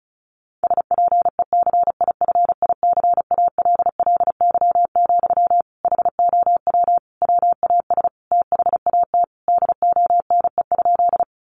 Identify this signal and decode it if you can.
Morse code, heard as a keyed tone, about 34 wpm, 700 Hz. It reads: SPECIFICALLY, HOW WAS THAT DONE?
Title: Sentences from Top 1100 Words 35wpm